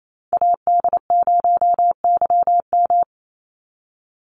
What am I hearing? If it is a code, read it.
Morse code, heard as a keyed tone, about 28 wpm, 700 Hz. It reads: AD0YM
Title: NA SOTA Chaser Callsigns 28wpm